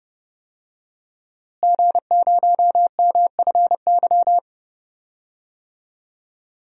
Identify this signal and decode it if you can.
Morse code, heard as a keyed tone, about 30 wpm, 700 Hz. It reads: G0MFY
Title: DXCC-100 Call Signs - 5 Characters - Repeated 3x - 30wpm